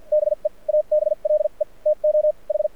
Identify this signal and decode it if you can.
Morse code, heard as a keyed tone, about 52 wpm, spaced as for 41 wpm, 600 Hz. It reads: BE ABLE TO H
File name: archived / CRSnet